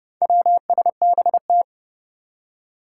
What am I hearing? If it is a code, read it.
Morse code, heard as a keyed tone, about 32 wpm, 700 Hz. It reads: WSBT
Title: Four Challenging Letter-Number Combinations 30wpm